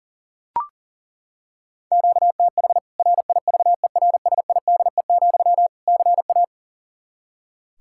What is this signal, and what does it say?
Morse code, heard as a keyed tone, about 40 wpm, 700 Hz. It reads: QTH RIVERSIDE, CA
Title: K1USN SST QSO Practice 40wpm